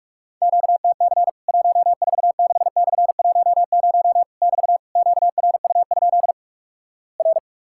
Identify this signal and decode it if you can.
Morse code, heard as a keyed tone, about 46 wpm, 700 Hz. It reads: QTC 146/10 <BT> QRU? R
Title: Worked All Europe DX Contest - QTC Exchange - 45wpm